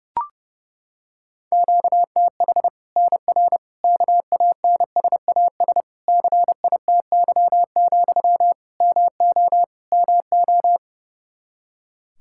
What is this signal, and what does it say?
Morse code, heard as a keyed tone, about 32 wpm, 700 Hz. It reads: QTH NR KANSAS CITY, MO MO
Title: Practice QSO - Exchange 1 - Beginning - 30wpm